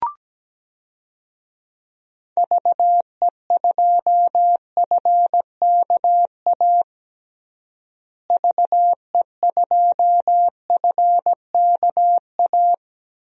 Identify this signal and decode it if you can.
Morse code, heard as a keyed tone, about 17 wpm, 700 Hz. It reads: VE2FKA VE2FKA
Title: DXCC-100 Call Signs - 6 Characters - Repeated 3x - 17wpm